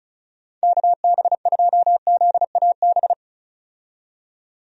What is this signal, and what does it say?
Morse code, heard as a keyed tone, about 35 wpm, 700 Hz. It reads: KB2ZAB
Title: USA Call Signs - 6 Characters - 35wpm